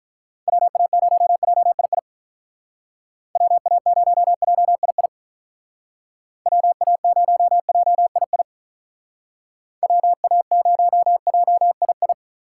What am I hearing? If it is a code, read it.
Morse code, heard as a keyed tone, about 45 wpm, 700 Hz. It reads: WA0JII WA0JII WA0JII WA0JII